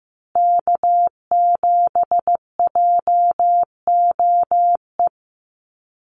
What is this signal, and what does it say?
Morse code, heard as a keyed tone, about 15 wpm, 700 Hz. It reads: K7JOE
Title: Call Signs Repeated 2x - 15wpm